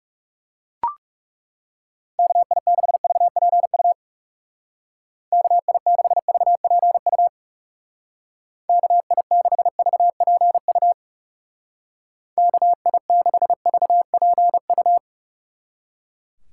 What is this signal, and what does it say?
Morse code, heard as a keyed tone, about 37 wpm, 700 Hz. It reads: KI6VPU KI6VPU KI6VPU KI6VPU
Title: Speed Racing - USA 2x3 Call Signs 30wpm